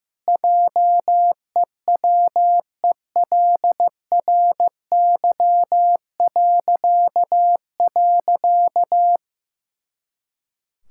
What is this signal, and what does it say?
Morse code, heard as a keyed tone, about 15 wpm, 700 Hz. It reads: JEWELRY..
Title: Pangram Challenge v2 - 15wpm